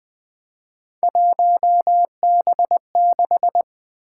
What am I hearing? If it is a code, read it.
Morse code, heard as a keyed tone, about 20 wpm, 700 Hz. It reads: 1B6
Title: Three Letter-Number Combinations - 20wpm